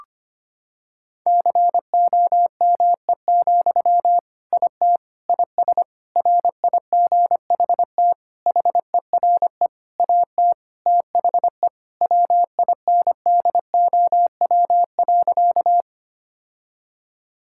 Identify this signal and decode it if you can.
Morse code, heard as a keyed tone, about 25 wpm, 700 Hz. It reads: COME, IT IS RIGHT HERE AT THE WINDOW.
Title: Sentences from Top 1600 Words 25wpm